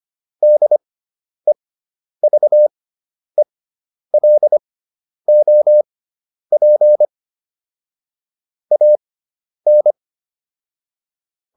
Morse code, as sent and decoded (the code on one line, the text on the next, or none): -.. ...- . .-.. --- .--. / .- -.
DEVELOP AN